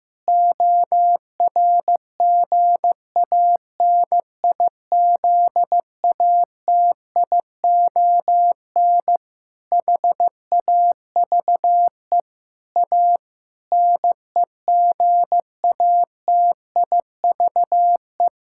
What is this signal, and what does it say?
Morse code, heard as a keyed tone, about 15 wpm, 700 Hz. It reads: ORGANIZATION HAVE A NEGATIVE
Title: Sentences from Top 1000 Words - Review Part 6 - 15wpm